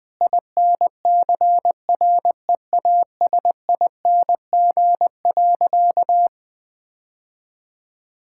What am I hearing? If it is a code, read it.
Morse code, heard as a keyed tone, about 20 wpm, 700 Hz. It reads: INCREASING.
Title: Sentences with 2-Word Phrases - 20wpm